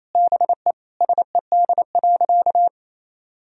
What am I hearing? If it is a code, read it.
Morse code, heard as a keyed tone, about 28 wpm, 700 Hz. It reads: BE SED.
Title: Sentences from Top 500 Words and Abbreviations 28wpm